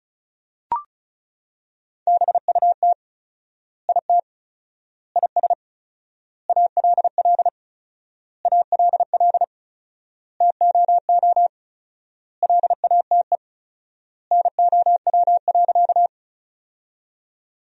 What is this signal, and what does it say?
Morse code, heard as a keyed tone, about 34 wpm, 700 Hz. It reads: BUT IT IS ALL ALL TOO LATE NOW.
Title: 4x Word-Spacing - Sentences from Top 300 Words - 35wpm